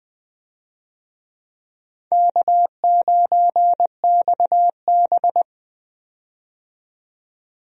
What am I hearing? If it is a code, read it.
Morse code, heard as a keyed tone, about 21 wpm, 700 Hz. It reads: K9XB
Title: USA Call Signs - 4 Characters - 20wpm